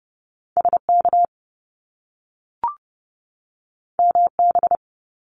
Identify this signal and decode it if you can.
Morse code, heard as a keyed tone, about 30 wpm, 700 Hz. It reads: SK MB
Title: Canadian Provinces and Territories 30wpm